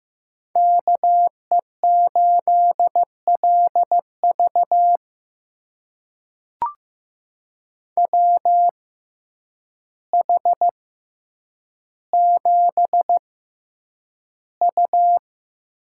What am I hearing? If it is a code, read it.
Morse code, heard as a keyed tone, about 15 wpm, spaced as for 8 wpm, 700 Hz. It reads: KE8LV WH7U
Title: USA Call Signs - 5 Characters - 6x Character Spacing - 15wpm